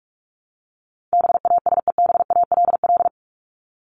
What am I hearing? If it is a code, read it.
Morse code, heard as a keyed tone, about 44 wpm, 700 Hz. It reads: BASEBALL
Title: Top 1751-2000 Words 45wpm